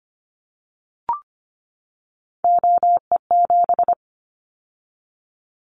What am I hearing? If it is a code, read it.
Morse code, heard as a keyed tone, about 25 wpm, 700 Hz. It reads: OE7
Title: Callsign Prefixes for Top 100 DXCC - 25wpm